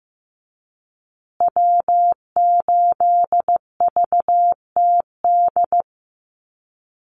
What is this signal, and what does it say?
Morse code, heard as a keyed tone, about 15 wpm, 700 Hz. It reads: W8VTD
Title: USA Call Signs Repeated 3x 15wpm